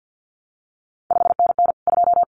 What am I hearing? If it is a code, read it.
Morse code, heard as a keyed tone, about 50 wpm, 700 Hz. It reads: 5NN 2
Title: CQ WW DX Contest Exchange - 50wpm